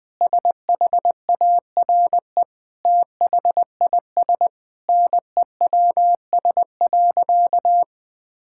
Morse code, heard as a keyed tone, about 20 wpm, 700 Hz. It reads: SHARE THIS NEWS.
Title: Sentences with 8-Word Phrases - 20wpm